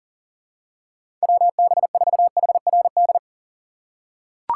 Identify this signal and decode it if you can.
Morse code, heard as a keyed tone, about 37 wpm, 700 Hz. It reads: WB4HRD
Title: USA Call Signs - 2x Character Spacing - 40wpm